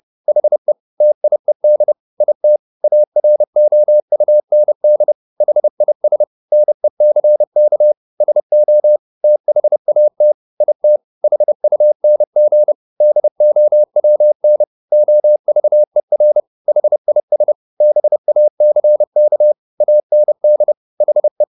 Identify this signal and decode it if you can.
Morse code, heard as a keyed tone, about 30 wpm, 600 Hz. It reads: HE TIED IT AROUND HIS NECK SO THAT IT HUNG DOWN OVER HIS BACK AND HE
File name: the old man and the sea / the old man and the sea 1 30wpm